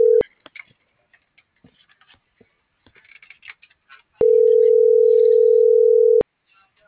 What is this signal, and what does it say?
Ringback tone (North America), 440 + 480 Hz.